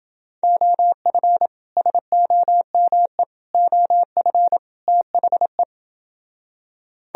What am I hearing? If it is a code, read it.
Morse code, heard as a keyed tone, about 29 wpm, 700 Hz. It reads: OF SOME OF THE